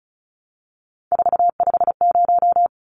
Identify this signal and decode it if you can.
Morse code, heard as a keyed tone, about 34 wpm, 700 Hz. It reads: 450
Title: Numbers - 3-Digit 35wpm